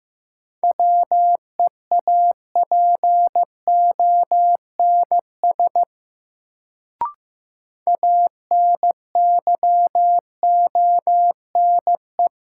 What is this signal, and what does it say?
Morse code, heard as a keyed tone, about 15 wpm, 700 Hz. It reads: WEAPONS ANYONE
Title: Sets of 5 Words Encore - New Word Drill - 15wpm